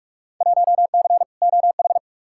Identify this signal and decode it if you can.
Morse code, heard as a keyed tone, about 42 wpm, 700 Hz. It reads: 1C OH
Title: Field Day Exchanges - 45wpm